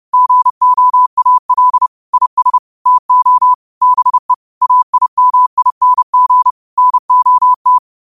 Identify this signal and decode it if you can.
Morse code, heard as a keyed tone, about 30 wpm, 1000 Hz. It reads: GOAL IS TO BE AIMING NOT